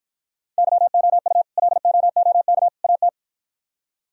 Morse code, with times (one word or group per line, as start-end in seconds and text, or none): 0.6-1.4 YOU
1.6-2.7 LOOK
2.8-3.1 AT